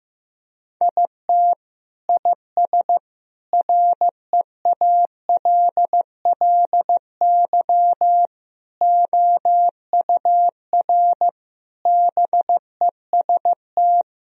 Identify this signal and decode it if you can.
Morse code, heard as a keyed tone, about 16 wpm, 700 Hz. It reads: IT IS REALLY OUR BEST
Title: Sentences from Top 1000 Words - Review Part 2 - 15wpm